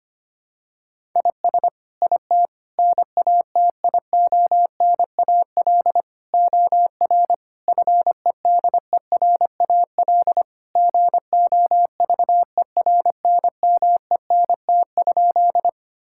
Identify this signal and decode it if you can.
Morse code, heard as a keyed tone, about 25 wpm, 700 Hz. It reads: IS IT NATIONAL OR FEDERAL GOVERNMENT?
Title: Sentences from Top 1000 Words - Review Part 5 - 25wpm